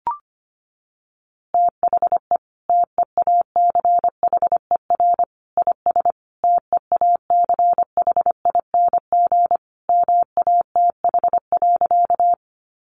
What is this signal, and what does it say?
Morse code, heard as a keyed tone, about 25 wpm, 700 Hz. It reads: THE TEACHER IS TEACHING MATH.
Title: Easy Phrases 25wpm